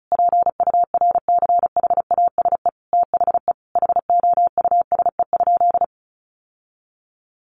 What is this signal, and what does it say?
Morse code, heard as a keyed tone, about 34 wpm, 700 Hz. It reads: PURCHASE THE HOUSE?
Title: Sentences from Top 800 Words 35wpm